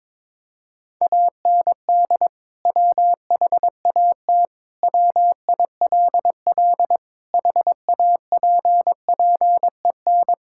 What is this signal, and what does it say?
Morse code, heard as a keyed tone, about 21 wpm, 700 Hz. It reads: AND WHAT WILL HAPPEN
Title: Sentences from Top 1000 Words - Review Part 5 - 22wpm